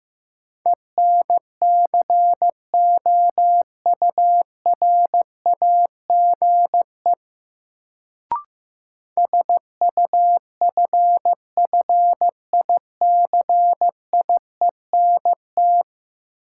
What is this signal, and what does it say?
Morse code, heard as a keyed tone, about 15 wpm, 700 Hz. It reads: ENCOURAGE SUFFICIENT
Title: Sentences from Top 1400 Words - New Word Drill - 15wpm